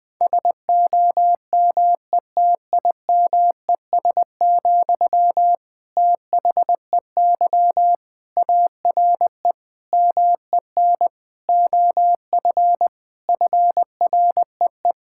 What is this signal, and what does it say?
Morse code, heard as a keyed tone, about 21 wpm, 700 Hz. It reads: SOMETIMES, THEY ARE MEN OF FREE
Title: Sentences from Top 1000 Words - Review Part 5 - 20wpm